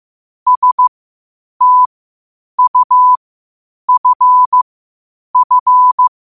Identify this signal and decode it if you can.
Morse code, heard as a keyed tone, about 15 wpm, spaced as for 8 wpm, 1000 Hz. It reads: STUFF
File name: qotdfiles / 8wpm